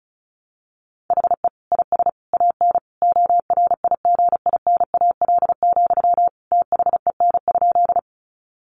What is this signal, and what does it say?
Morse code, heard as a keyed tone, about 34 wpm, 700 Hz. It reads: HE IS AN ORIGINAL, THEN?
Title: Sentences from Top 900 Words 35wpm